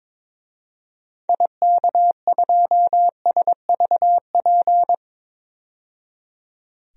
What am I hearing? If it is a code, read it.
Morse code, heard as a keyed tone, about 22 wpm, 700 Hz. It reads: IK2SVP
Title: DXCC-100 Call Signs - 6 Characters - 22wpm